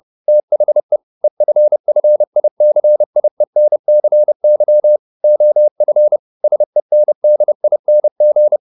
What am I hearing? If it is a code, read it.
Morse code, heard as a keyed tone, about 30 wpm, 600 Hz. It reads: THE EFFICIENCY OF SENDING